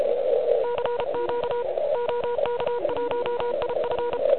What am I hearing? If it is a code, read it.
Morse code, heard as a keyed tone, about 34 wpm, 960 Hz, with noise. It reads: CQ OK1IF
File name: ARCHIV / QCX / QCX Pileup